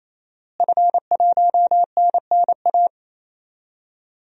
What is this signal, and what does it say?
Morse code, heard as a keyed tone, about 28 wpm, 700 Hz. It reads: F1NNA
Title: DXCC-100 Call Signs - 5 Characters - Repeated 3x - 28wpm